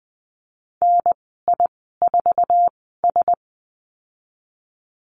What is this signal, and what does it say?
Morse code, heard as a keyed tone, about 20 wpm, spaced as for 15 wpm, 700 Hz. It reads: NI4S
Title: 048 Lesson 9g - Number 4 - Call Signs 20wpm